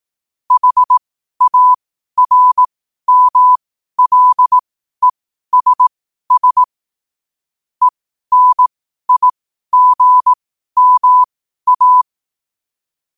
Morse code, heard as a keyed tone, about 18 wpm, spaced as for 12 wpm, 1000 Hz. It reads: HARMLESS ENIGMA
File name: qotdfiles / 12wpm